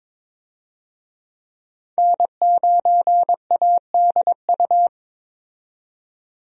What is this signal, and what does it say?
Morse code, heard as a keyed tone, about 22 wpm, 700 Hz. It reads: N9ADU
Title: USA Call Signs - 5 Characters - 22wpm